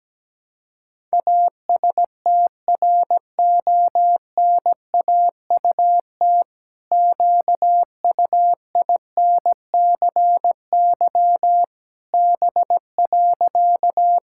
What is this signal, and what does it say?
Morse code, heard as a keyed tone, about 17 wpm, 700 Hz. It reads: ASTRONAUT QUINCY B.
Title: Pangram Challenge v2 - 17wpm